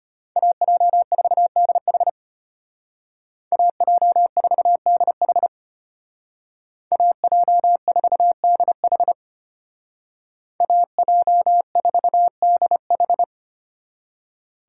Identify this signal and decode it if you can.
Morse code, heard as a keyed tone, about 30 wpm, 700 Hz. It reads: AJ4DH AJ4DH AJ4DH AJ4DH
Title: Speed Racing - USA 2x2 Call Signs 25wpm